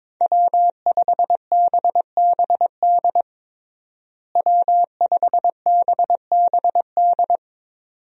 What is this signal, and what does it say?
Morse code, heard as a keyed tone, about 22 wpm, 700 Hz. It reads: W5BBD W5BBD
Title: USA Call Signs - 5 Characters - Repeated 2x - 22wpm